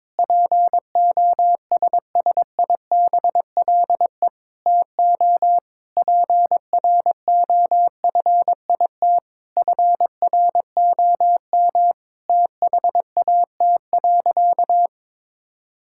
Morse code, heard as a keyed tone, about 22 wpm, 700 Hz. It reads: POSSIBLE TO PROFIT FROM THAT.
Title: Sentences from Top 1000 Words - Review Part 5 - 22wpm